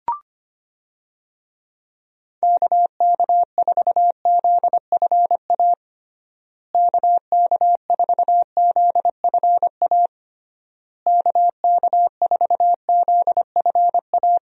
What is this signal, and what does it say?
Morse code, heard as a keyed tone, about 25 wpm, 700 Hz. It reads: KK4ZFA KK4ZFA KK4ZFA
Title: USA Call Signs - 6 Characters - Repeated 3x - 25wpm